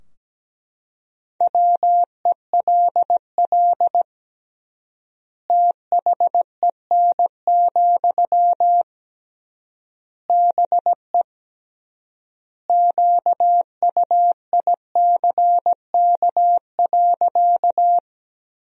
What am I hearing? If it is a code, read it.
Morse code, heard as a keyed tone, about 17 wpm, 700 Hz. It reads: WELL THEN, BE QUICK.
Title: 3x Word-Spacing - Sentences from Top 500 Words - 17wpm